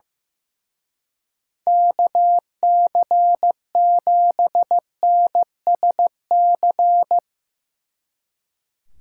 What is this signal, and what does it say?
Morse code, heard as a keyed tone, about 16 wpm, 700 Hz. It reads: KC7NSC
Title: Speed Racing - USA 2x3 Call Signs 15wpm